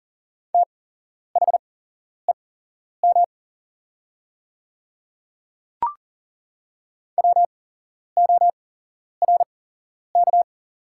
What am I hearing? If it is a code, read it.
Morse code, heard as a keyed tone, about 38 wpm, spaced as for 11 wpm, 700 Hz. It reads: THEM WORK
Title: 8x Character Spacing - Top 200 Words - 40wpm